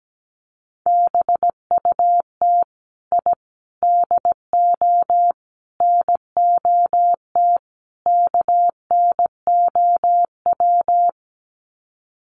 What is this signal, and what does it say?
Morse code, heard as a keyed tone, about 17 wpm, 700 Hz. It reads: BUT I DO NOT KNOW
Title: Sets of 5 Words 17wpm